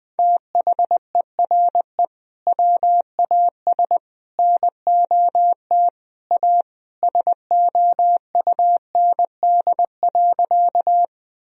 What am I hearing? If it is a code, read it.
Morse code, heard as a keyed tone, about 21 wpm, 700 Hz. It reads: THERE WAS NOT A SOUND.
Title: Sentences from Top 1000 Words - Review Part 1 - 20wpm